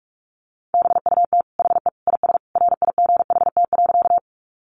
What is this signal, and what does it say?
Morse code, heard as a keyed tone, about 44 wpm, 700 Hz. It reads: BUT HE IS RIGHT.
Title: Sentences from Top 200 Words 45wpm